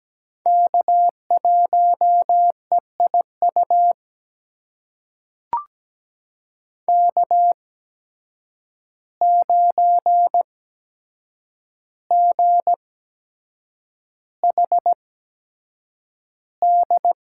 Morse code, heard as a keyed tone, about 17 wpm, spaced as for 7 wpm, 700 Hz. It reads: K1EIU K9GHD
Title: USA Call Signs - 5 Characters - 8x Character Spacing - 17wpm